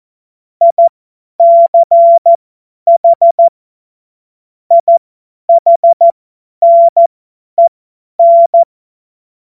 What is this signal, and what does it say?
Morse code, heard as a keyed tone, about 14 wpm, spaced as for 10 wpm, 680 Hz. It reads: ICH IHNEN
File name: Morsekurs / Witze / Witz 20. Dezember 2022 50 F70